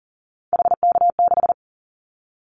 Morse code, heard as a keyed tone, about 40 wpm, 700 Hz. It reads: HK6
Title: Callsign Prefixes for Top 100 DXCC - 40wpm